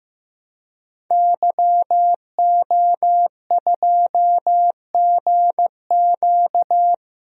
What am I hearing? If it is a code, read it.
Morse code, heard as a keyed tone, about 15 wpm, 700 Hz. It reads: YO2GQ
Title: DXCC-100 Call Signs - 5 Characters - Repeated 2x - 15wpm